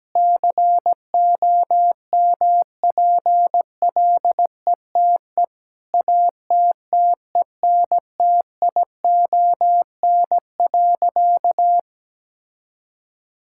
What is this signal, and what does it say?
Morse code, heard as a keyed tone, about 17 wpm, 700 Hz. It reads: COMPLETE ATTENTION.
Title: Sentences from Top 1000 Words - Review Part 6 - 17wpm